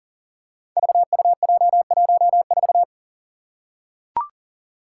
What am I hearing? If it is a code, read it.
Morse code, heard as a keyed tone, about 38 wpm, 700 Hz. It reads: VUJ14
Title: Five Challenging Letter-Number Combinations 40wpm